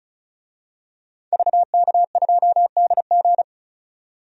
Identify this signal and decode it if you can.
Morse code, heard as a keyed tone, about 36 wpm, 700 Hz. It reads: VK2DG
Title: DXCC-100 Call Signs - 5 Characters - 35wpm